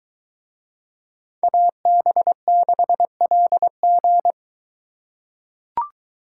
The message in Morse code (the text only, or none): AB6LG